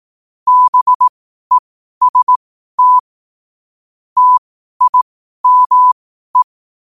Morse code, heard as a keyed tone, about 18 wpm, spaced as for 12 wpm, 1000 Hz. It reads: BEST TIME
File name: qotdfiles / 12wpm